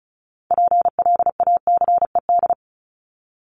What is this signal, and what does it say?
Morse code, heard as a keyed tone, about 34 wpm, 700 Hz. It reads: PLACED
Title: ED Suffix - Top 100 Words with Base Words 35wpm